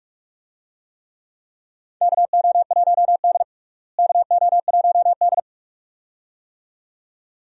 Morse code, heard as a keyed tone, about 48 wpm, 700 Hz. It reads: KO1D KO1D
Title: USA Call Signs - 4 Characters - Repeated 2x - 45wpm